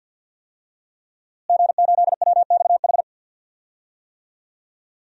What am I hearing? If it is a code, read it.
Morse code, heard as a keyed tone, about 47 wpm, 700 Hz. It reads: G8WXH
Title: DXCC-100 Call Signs - 5 Characters - Repeated 2x - 50wpm